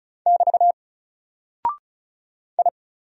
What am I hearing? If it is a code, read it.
Morse code, heard as a keyed tone, about 36 wpm, 700 Hz. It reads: <BT> I
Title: Single Character (Letter:Number:Punctuation:Prosign) Alt 35wpm